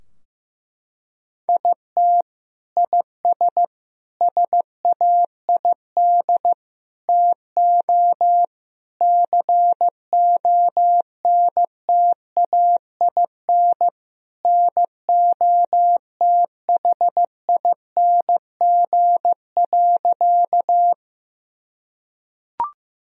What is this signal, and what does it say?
Morse code, heard as a keyed tone, about 15 wpm, 700 Hz. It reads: IT IS SAID TO CONTAIN NOTHING.
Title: Sentences from Top 1000 Words - Review Part 2 - 15wpm